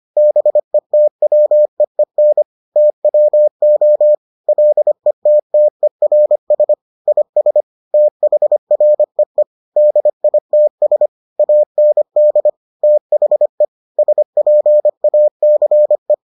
Morse code, heard as a keyed tone, about 25 wpm, 600 Hz. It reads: BETWEEN TWO LETTERS IS THREE DITS AND THE SPACE